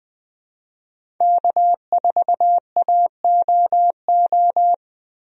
Morse code, heard as a keyed tone, about 21 wpm, 700 Hz. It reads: K4AOO
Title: USA Call Signs - 5 Characters - Repeated 3x - 20wpm